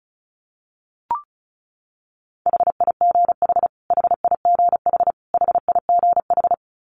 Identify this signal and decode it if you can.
Morse code, heard as a keyed tone, about 34 wpm, 700 Hz. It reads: HIGH HIGH HIGH
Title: Top 500 Words Repeated 35wpm